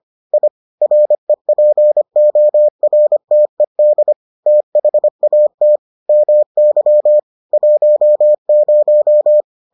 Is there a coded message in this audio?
Yes: Morse code from 0.3 to 9.4 s, heard as a keyed tone, about 25 wpm, 600 Hz. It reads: I REPORTED THAT MY 10